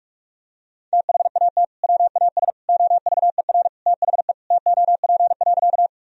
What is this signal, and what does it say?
Morse code, heard as a keyed tone, about 41 wpm, 700 Hz. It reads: THAT WAS OVER THE TOP.
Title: Sentences from Top 1000 Words - Review Part 2 - 45wpm